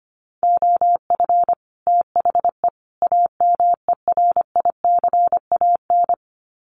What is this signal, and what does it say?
Morse code, heard as a keyed tone, about 25 wpm, 700 Hz. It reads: OF THE AMERICAN
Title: Sets of 3 Words 25wpm